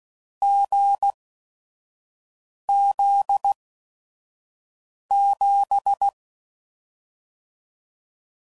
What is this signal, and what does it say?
Morse code, heard as a keyed tone, about 16 wpm, spaced as for 5 wpm, 800 Hz. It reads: GZ7